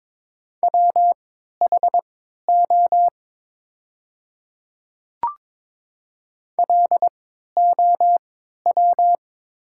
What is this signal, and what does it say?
Morse code, heard as a keyed tone, about 22 wpm, spaced as for 12 wpm, 700 Hz. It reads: WHO LOW
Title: 3x Character Spacing - Top 300 Words - 22wpm